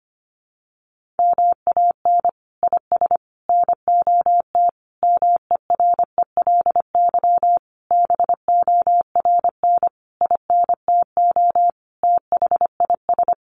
Morse code, heard as a keyed tone, about 25 wpm, 700 Hz. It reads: MAN IS NOT MERELY BORN INTO THIS
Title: Sentences from Top 1400 Words 25wpm